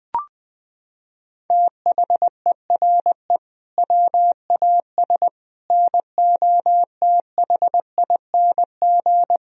Morse code, heard as a keyed tone, about 20 wpm, 700 Hz. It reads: THERE WAS NOTHING
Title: Sentences from Top 1000 Words - Review Part 2 - 20wpm